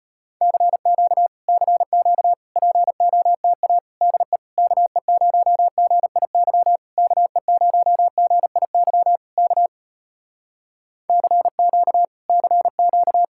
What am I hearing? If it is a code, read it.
Morse code, heard as a keyed tone, about 33 wpm, 700 Hz. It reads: CQ CQ POTA DE KE0GIY KE0GIY K CQ CQ